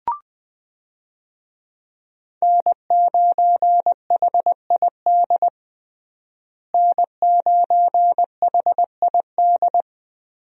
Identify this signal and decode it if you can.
Morse code, heard as a keyed tone, about 21 wpm, 700 Hz. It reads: N9HID N9HID
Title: USA Call Signs - 5 Characters - Repeated 2x - 20wpm